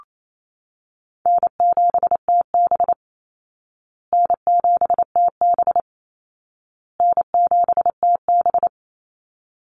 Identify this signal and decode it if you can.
Morse code, heard as a keyed tone, about 28 wpm, 700 Hz. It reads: N7TB N7TB N7TB
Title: Call Signs Repeated 2x - 28wpm